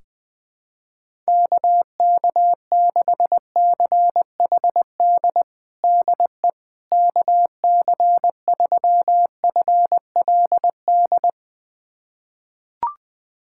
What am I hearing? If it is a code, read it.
Morse code, heard as a keyed tone, about 20 wpm, 700 Hz. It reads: KK6CHD DE KC3FLD
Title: USA Callsign DE Callsign - 6 Characters - 20wpm